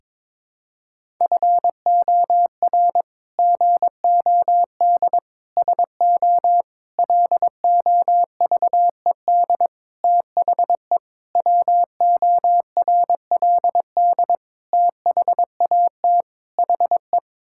Morse code, heard as a keyed tone, about 22 wpm, 700 Hz. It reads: FOR GOD SO LOVED THE WORLD THAT HE
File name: practice / 22wpm / Sets of 8 Words - 22wpm